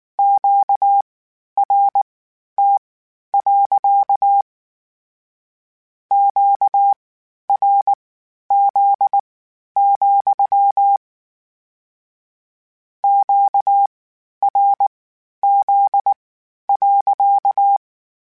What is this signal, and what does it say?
Morse code, heard as a keyed tone, about 19 wpm, spaced as for 11 wpm, 800 Hz. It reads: QRT. QRZ, QRZ.